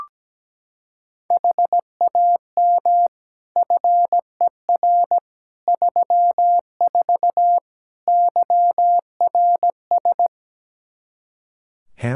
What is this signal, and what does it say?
Morse code, heard as a keyed tone, about 17 wpm, 700 Hz. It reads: HAM FER 34 YRS
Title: Practice QSO - Exchange 3 - More Info - 17wpm